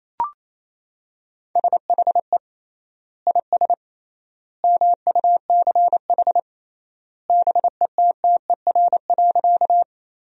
Morse code, heard as a keyed tone, about 28 wpm, 700 Hz. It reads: SHE IS MUCH BETTER.
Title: 3x Word-Spacing - Sentences from Top 400 Words - 28wpm